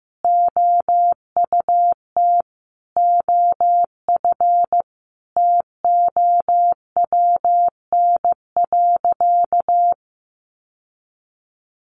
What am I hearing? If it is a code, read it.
Morse code, heard as a keyed tone, about 15 wpm, 700 Hz. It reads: OUT OF TOWN.
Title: Sentences from Top 1300 Words 15wpm